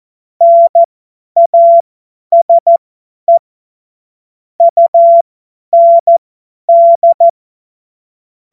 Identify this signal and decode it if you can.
Morse code, heard as a keyed tone, about 14 wpm, spaced as for 10 wpm, 680 Hz. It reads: NASE UND
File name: Morsekurs / Witze / Witz 12. April 2022 50 F70